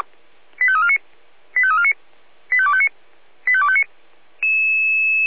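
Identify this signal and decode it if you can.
Five-tone selective call (ZVEI), heard as quick runs of five tones, tones of about 70 ms. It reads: 85349 85349 85259 85259
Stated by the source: ZVEI five-tone selective call received on an SDR.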